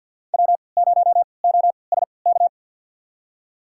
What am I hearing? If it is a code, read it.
Morse code, heard as a keyed tone, about 50 wpm, spaced as for 28 wpm, 700 Hz. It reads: W0OSK
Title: USA Call Signs - 5 Characters - 3x Character Spacing - 50wpm